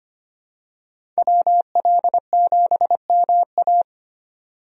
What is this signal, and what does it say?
Morse code, heard as a keyed tone, about 25 wpm, 700 Hz. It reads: WL7MA
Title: USA Call Signs - 5 Characters - 25wpm